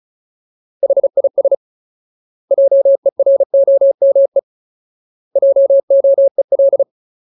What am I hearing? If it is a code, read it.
Morse code, heard as a keyed tone, about 34 wpm, 550 Hz. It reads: HIS JEROME JOEL
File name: Ultimate Word List / 35WPM Ultimate